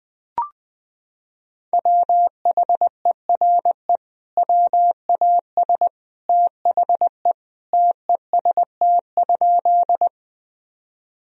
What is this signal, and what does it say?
Morse code, heard as a keyed tone, about 21 wpm, 700 Hz. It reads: WHERE WAS THE TEST?
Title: Sentences from Top 1000 Words - Review Part 3 - 20wpm